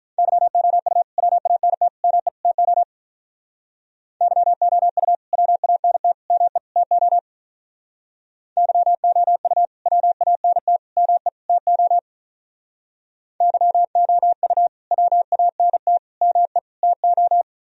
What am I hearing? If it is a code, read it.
Morse code, heard as a keyed tone, about 42 wpm, 700 Hz. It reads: YOU WANT ME TO YOU WANT ME TO YOU WANT ME TO YOU WANT ME TO
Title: Speed-Racing - Sets of 4 Words 35wpm